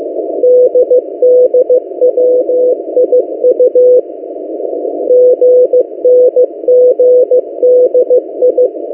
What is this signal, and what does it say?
Morse code, heard as a keyed tone, about 15 wpm, 500 Hz. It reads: DDWIU GNGDI
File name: M08a- -1355utc-8096khz